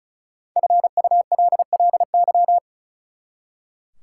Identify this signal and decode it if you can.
Morse code, heard as a keyed tone, about 36 wpm, 700 Hz. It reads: FULLY
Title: Five Letter Words - 35wpm